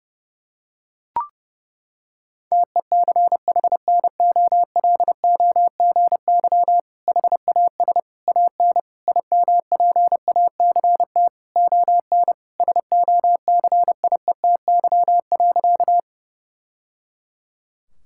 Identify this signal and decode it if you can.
Morse code, heard as a keyed tone, about 30 wpm, 700 Hz. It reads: TECHNOLOGY HAS AN IMPACT ON SOCIETY.
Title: Sentences from Top 1000 Words - Review Part 6 - 30wpm